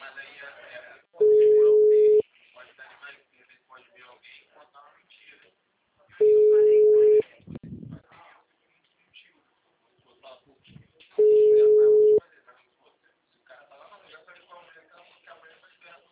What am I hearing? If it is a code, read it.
Ringback tone (Europe, 425 Hz), with speech.